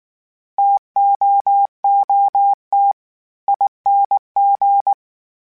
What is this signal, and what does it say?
Morse code, heard as a keyed tone, about 19 wpm, 800 Hz. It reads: TOOT ING
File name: Variable Speed / Variable Speed-mp3s / TO sp @18